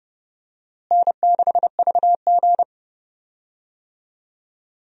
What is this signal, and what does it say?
Morse code, heard as a keyed tone, about 31 wpm, 700 Hz. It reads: N6VG
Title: USA Call Signs - 4 Characters - Repeated 3x - 30wpm